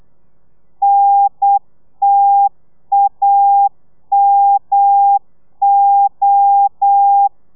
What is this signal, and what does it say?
Morse code, heard as a keyed tone, about 8 wpm, 800 Hz. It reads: NTAMO